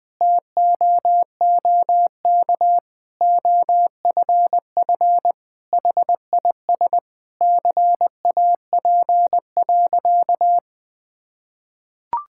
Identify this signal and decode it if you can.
Morse code, heard as a keyed tone, about 20 wpm, 700 Hz. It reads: TOOK OFF HIS CAP.
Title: Sentences from Top 1600 Words 20wpm